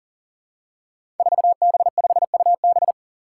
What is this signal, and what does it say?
Morse code, heard as a keyed tone, about 40 wpm, 700 Hz. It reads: 4B5UB
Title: Five Challenging Letter-Number Combinations 40wpm